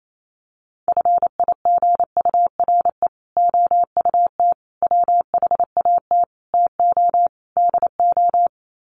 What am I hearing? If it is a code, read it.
Morse code, heard as a keyed tone, about 28 wpm, 700 Hz. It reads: FIGURE OUT WHAT TO DO
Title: Sets of 5 Words Encore2 28wpm